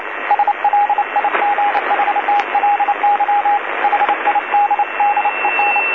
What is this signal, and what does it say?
Morse code, heard as a keyed tone, about 29 wpm, 820 Hz, with radio noise. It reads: SLIGHTLY HIDDEN
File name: DC-TRCVR / 80-CW